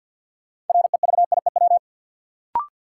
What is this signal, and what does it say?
Morse code, heard as a keyed tone, about 53 wpm, 700 Hz. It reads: REVIEW
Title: Sentences from Top 900 Words - New Word Drill - 50wpm